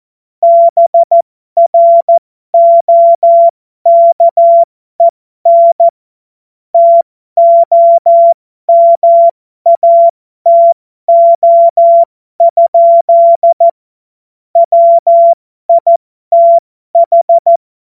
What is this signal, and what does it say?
Morse code, heard as a keyed tone, about 14 wpm, spaced as for 12 wpm, 680 Hz. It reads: BROKEN TOMATO? WITH
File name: Morsekurs / Witze